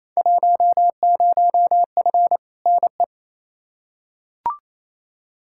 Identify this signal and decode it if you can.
Morse code, heard as a keyed tone, about 28 wpm, 700 Hz. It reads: 10F NE
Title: Field Day Exchanges Encore - 28wpm